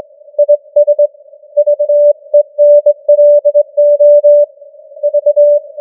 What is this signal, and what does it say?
Morse code, heard as a keyed tone, about 22 wpm, 590 Hz. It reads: IS VENLO V
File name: challenge-178